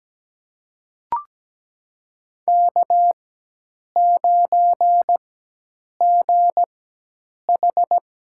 Morse code, heard as a keyed tone, about 17 wpm, spaced as for 8 wpm, 700 Hz. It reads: K9GH
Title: USA Call Signs - 4x Character Spacing - 17wpm